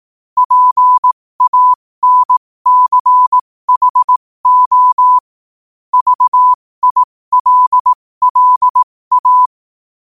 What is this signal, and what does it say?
Morse code, heard as a keyed tone, about 18 wpm, spaced as for 15 wpm, 1000 Hz. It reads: PANCHO VILLA